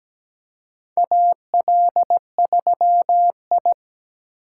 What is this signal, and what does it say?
Morse code, heard as a keyed tone, about 17 wpm, 700 Hz. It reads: AL3I
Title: USA Call Signs - 4 Characters - 4x Character Spacing - 17wpm